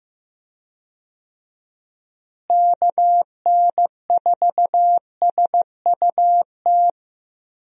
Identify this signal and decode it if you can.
Morse code, heard as a keyed tone, about 16 wpm, 700 Hz. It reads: KN4SUT
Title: USA Call Signs - 6 Characters - Repeated 2x - 15wpm